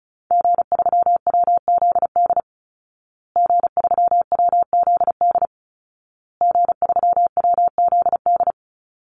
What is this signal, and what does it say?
Morse code, heard as a keyed tone, about 34 wpm, 700 Hz. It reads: G3WZD G3WZD G3WZD
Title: Call Signs Repeated 2x - 35wpm